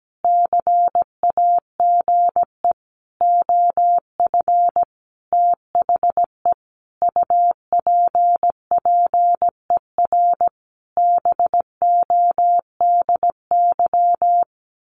Morse code, heard as a keyed tone, about 17 wpm, 700 Hz. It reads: CAGE OF THE UPPER BODY
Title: A Sound Of Thunder 17wpm